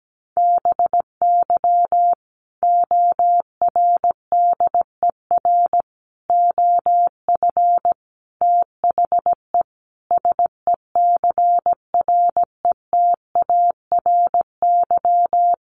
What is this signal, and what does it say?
Morse code, heard as a keyed tone, about 17 wpm, 700 Hz. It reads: BY ORDER OF THE SECRETARY
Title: Sentences from Top 1300 Words 17wpm